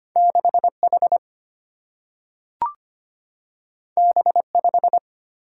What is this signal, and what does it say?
Morse code, heard as a keyed tone, about 25 wpm, 700 Hz. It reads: 6H B5
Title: Single-Letter Single-Number 25wpm